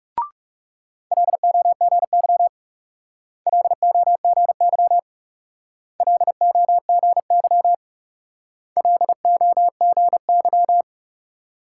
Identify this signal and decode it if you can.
Morse code, heard as a keyed tone, about 38 wpm, 700 Hz. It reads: LOGY LOGY LOGY LOGY